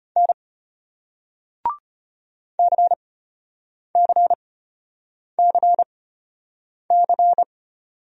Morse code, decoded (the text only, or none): N CCCC